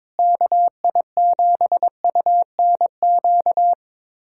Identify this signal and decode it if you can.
Morse code, heard as a keyed tone, about 22 wpm, 700 Hz. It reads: KI7UNQ
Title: USA Callsign DE Callsign - 6 Characters - 22wpm